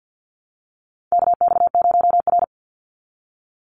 Morse code, heard as a keyed tone, about 50 wpm, 700 Hz. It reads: KX0R
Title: NA SOTA Chaser Callsigns 50wpm